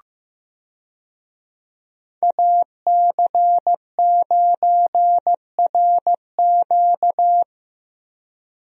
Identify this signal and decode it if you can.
Morse code, heard as a keyed tone, about 15 wpm, 700 Hz. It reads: AC9RQ
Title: USA Call Signs - 5 Characters - Repeated 2x - 15wpm